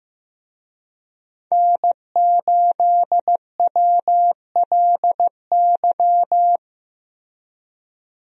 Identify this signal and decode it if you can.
Morse code, heard as a keyed tone, about 15 wpm, 700 Hz. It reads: N8WLY
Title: USA Call Signs - 5 Characters - Repeated 3x - 15wpm